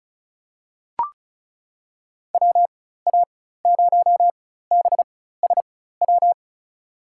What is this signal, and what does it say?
Morse code, heard as a keyed tone, about 36 wpm, spaced as for 16 wpm, 700 Hz. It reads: WA0BSW
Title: USA Call Signs - 4x Character Spacing - 35wpm